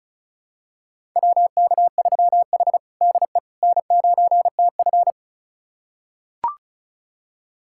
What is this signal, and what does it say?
Morse code, heard as a keyed tone, about 36 wpm, 700 Hz. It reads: WK3H DE N9TF
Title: USA Callsign DE Callsign - 4 Characters - 35wpm